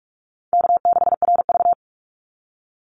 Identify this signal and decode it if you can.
Morse code, heard as a keyed tone, about 44 wpm, 700 Hz. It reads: K6RV
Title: Easy Call Signs - 45wpm